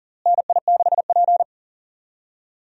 Morse code, heard as a keyed tone, about 40 wpm, 700 Hz. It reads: NI6P
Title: USA Call Signs - 4 Characters - Repeated 2x - 40wpm